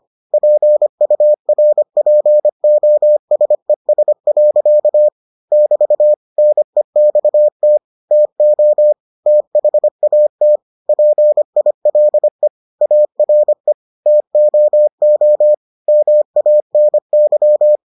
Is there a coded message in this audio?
Yes: Morse code from 0.3 to 17.8 s, heard as a keyed tone, about 25 wpm, 600 Hz. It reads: PURPOSES. <BT> NEXT TO THAT PILE ARE TOO MANY